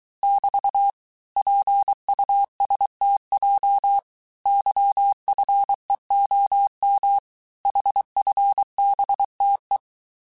Morse code, heard as a keyed tone, about 23 wpm, 800 Hz. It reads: <BT> PUSTJ YFEOM HFBTE
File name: cours-du-mercredi / audios / cours-855-texte-chiffre-vitesse-22-mpm